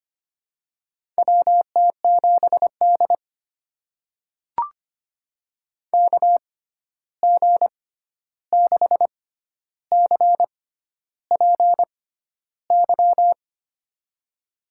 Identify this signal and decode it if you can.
Morse code, heard as a keyed tone, about 25 wpm, spaced as for 11 wpm, 700 Hz. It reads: WT7D KG6CPY
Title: USA Call Signs - 6x Character Spacing - 25wpm